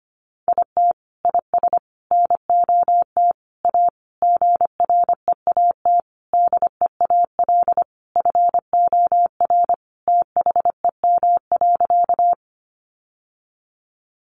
Morse code, heard as a keyed tone, about 25 wpm, 700 Hz. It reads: IT IS NOT A GREAT DEAL FOR THEM.
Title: Sentences from Top 800 Words 25wpm